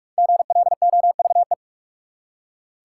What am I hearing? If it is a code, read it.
Morse code, heard as a keyed tone, about 46 wpm, 700 Hz. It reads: GLOVE
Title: Five Letter Words - 45wpm